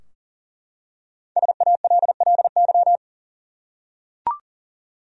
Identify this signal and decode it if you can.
Morse code, heard as a keyed tone, about 39 wpm, 700 Hz. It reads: SALLY
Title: USA Common Names - 40wpm